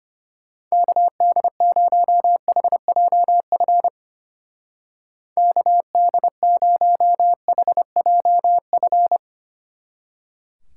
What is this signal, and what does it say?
Morse code, heard as a keyed tone, about 31 wpm, 700 Hz. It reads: KD0HJF KD0HJF
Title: Speed Racing - USA 2x3 Call Signs 25wpm